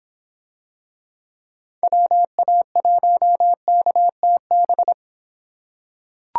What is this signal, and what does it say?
Morse code, heard as a keyed tone, about 24 wpm, 700 Hz. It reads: WA1KTB